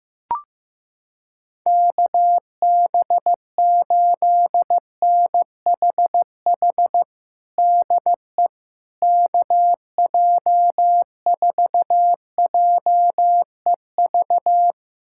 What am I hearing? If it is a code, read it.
Morse code, heard as a keyed tone, about 15 wpm, 700 Hz. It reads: KB8NHH DE KJ4JEV
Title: USA Callsign DE Callsign 15wpm